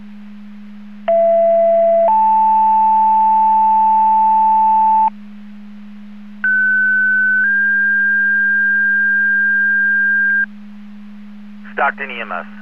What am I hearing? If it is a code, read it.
Two-tone page, heard as a one-second tone, then a three-second tone, about 670 then 880 Hz, about 1500 then 1600 Hz, with speech. It is heard from 1.1 to 10.4 s.